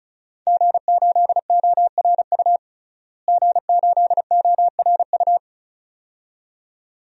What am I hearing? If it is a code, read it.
Morse code, heard as a keyed tone, about 36 wpm, 700 Hz. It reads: G8ORU G8ORU
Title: DXCC-100 Call Signs - 5 Characters - Repeated 2x - 35wpm